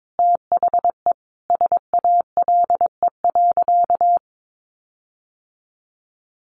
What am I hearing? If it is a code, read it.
Morse code, heard as a keyed tone, about 22 wpm, 700 Hz. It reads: THE SALE.
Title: Sentences from Top 1400 Words 22wpm